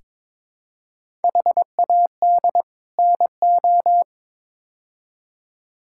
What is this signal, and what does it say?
Morse code, heard as a keyed tone, about 22 wpm, 700 Hz. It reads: HAD NO
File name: practice / 22wpm / Build A Phrase - 5-Word Phrases - 22wpm